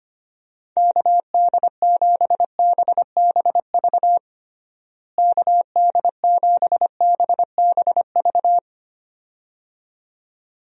Morse code, heard as a keyed tone, about 25 wpm, 700 Hz. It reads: KD7BBV KD7BBV
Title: USA Call Signs - 6 Characters - Repeated 2x - 25wpm